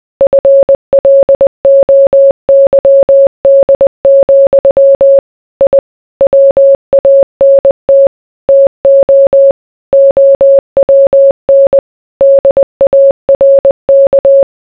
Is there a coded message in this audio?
Yes: Morse code from 0.2 to 14.4 s, heard as a keyed tone, about 20 wpm, 550 Hz. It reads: FLOYD, I WANT TO OWN DARK